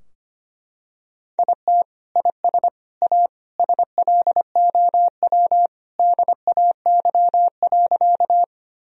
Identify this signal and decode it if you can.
Morse code, heard as a keyed tone, about 25 wpm, 700 Hz. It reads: IT IS A SLOW DAY.
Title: Sentences from Top 1000 Words - Review Part 2 - 25wpm